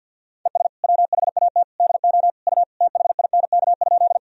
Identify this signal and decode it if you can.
Morse code, heard as a keyed tone, about 48 wpm, 700 Hz. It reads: ES WHAT DO U THINK?
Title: Sentences from Top 200 Words and Abbreviations 50wpm